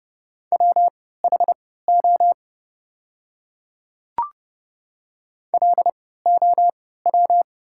Morse code, heard as a keyed tone, about 30 wpm, spaced as for 17 wpm, 700 Hz. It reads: WHO LOW
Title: 3x Character Spacing - Top 300 Words - 30wpm